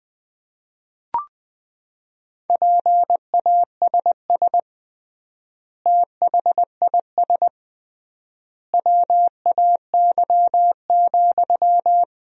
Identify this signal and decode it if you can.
Morse code, heard as a keyed tone, about 20 wpm, 700 Hz. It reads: PASS THIS WAY,
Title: 3x Word-Spacing - Sentences from Top 400 Words - 20wpm